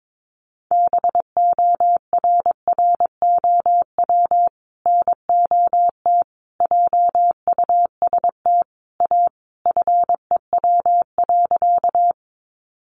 Morse code, heard as a keyed tone, about 22 wpm, 700 Hz. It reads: BORROW NOT JUST A FEW.
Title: Sentences from Top 1500 Words 22wpm